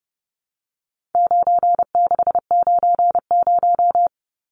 Morse code, heard as a keyed tone, about 30 wpm, 700 Hz. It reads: 9690
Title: Numbers - 4-Digit 30wpm